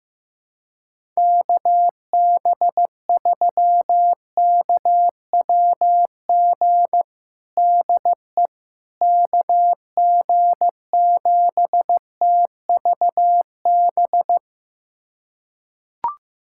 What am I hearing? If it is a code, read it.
Morse code, heard as a keyed tone, about 15 wpm, 700 Hz. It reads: KB3KWG DE KG7TVB
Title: USA Callsign DE Callsign 15wpm